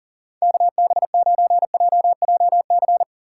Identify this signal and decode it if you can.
Morse code, heard as a keyed tone, about 40 wpm, 700 Hz. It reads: KB9JJC
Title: USA Call Signs - 2x Character Spacing - 40wpm